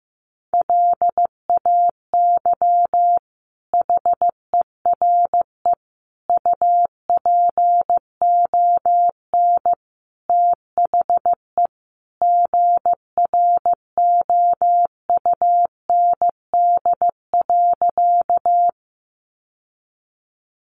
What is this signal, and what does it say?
Morse code, heard as a keyed tone, about 15 wpm, 700 Hz. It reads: LAY HERE UPON THE GROUND.
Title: Sentences from Top 1100 Words 15wpm